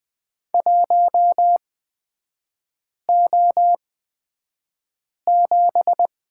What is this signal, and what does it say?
Morse code, heard as a keyed tone, about 20 wpm, spaced as for 5 wpm, 700 Hz. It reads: 1O7
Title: 224 Lesson 36e - BT - Three Characters 20wpm